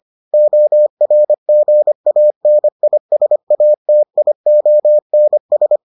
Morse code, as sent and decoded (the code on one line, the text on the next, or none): --- .-. --. .- -. .. ... .- - .. --- -. ...
ORGANISATIONS